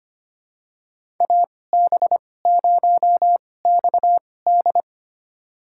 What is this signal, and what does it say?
Morse code, heard as a keyed tone, about 25 wpm, spaced as for 18 wpm, 700 Hz. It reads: AB0XD
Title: USA Call Signs - 2x Character Spacing - 25wpm